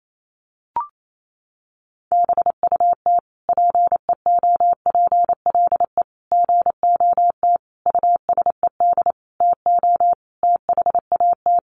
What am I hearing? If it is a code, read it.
Morse code, heard as a keyed tone, about 28 wpm, 700 Hz. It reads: BUT PEOPLE GOT USED TO THAT
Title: Sentences from Top 700 Words 28wpm